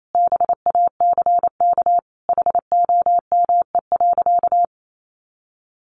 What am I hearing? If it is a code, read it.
Morse code, heard as a keyed tone, about 28 wpm, 700 Hz. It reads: BACK HOME.
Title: Sentences from Top 500 Words 28wpm